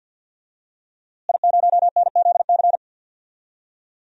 Morse code, heard as a keyed tone, about 51 wpm, 700 Hz. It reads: I0NZC
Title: DXCC-100 Call Signs - 5 Characters - Repeated 3x - 50wpm